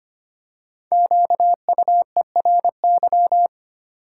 Morse code, heard as a keyed tone, about 25 wpm, 700 Hz. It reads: QUERY
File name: 5 Letter Words - First 3 Letters Given - 25wpm